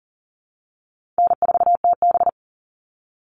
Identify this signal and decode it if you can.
Morse code, heard as a keyed tone, about 40 wpm, 700 Hz. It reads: N4TB
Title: Easy Call Signs - 40wpm